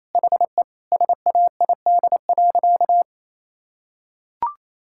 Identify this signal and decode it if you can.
Morse code, heard as a keyed tone, about 28 wpm, 700 Hz. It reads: HE SAID.
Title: Sentences from Top 1000 Words - Review Part 6 - 28wpm